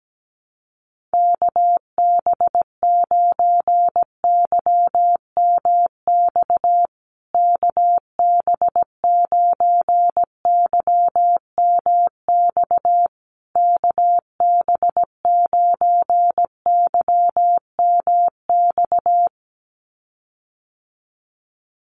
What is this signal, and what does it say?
Morse code, heard as a keyed tone, about 17 wpm, 700 Hz. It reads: KB9YMX KB9YMX KB9YMX
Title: USA Call Signs Repeated 3x 17wpm